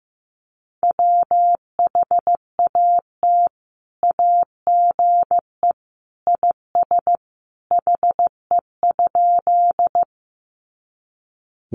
Morse code, heard as a keyed tone, about 15 wpm, 700 Hz. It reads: WHAT AGE IS HE?
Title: Sentences from Top 500 Words 15wpm